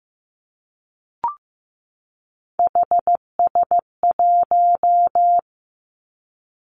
Morse code, heard as a keyed tone, about 15 wpm, 700 Hz. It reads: HS1
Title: Three Challenging Letter-Number Combinations 15wpm